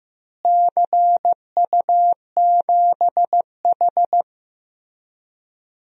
Morse code, heard as a keyed tone, about 15 wpm, 700 Hz. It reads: CU7H
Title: DXCC-100 Call Signs - 4 Characters - Repeated 3x - 15wpm